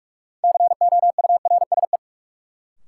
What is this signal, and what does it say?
Morse code, heard as a keyed tone, about 46 wpm, 700 Hz. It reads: COURSE
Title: Sentences from Top 500 Words - New Word Drill - 45wpm